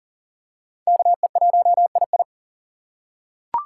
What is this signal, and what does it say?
Morse code, heard as a keyed tone, about 39 wpm, 700 Hz. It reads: KE1II
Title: USA Call Signs - 5 Characters - 2x Character Spacing - 40wpm